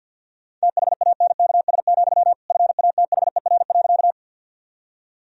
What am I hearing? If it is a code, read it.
Morse code, heard as a keyed tone, about 50 wpm, 700 Hz. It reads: THANKS, FATHER.
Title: Sentences from Top 1000 Words - Review Part 3 - 50wpm